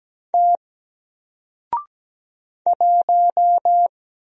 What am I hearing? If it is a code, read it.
Morse code, heard as a keyed tone, about 17 wpm, 700 Hz. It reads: T 1
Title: Single Letter-Number Alt 17wpm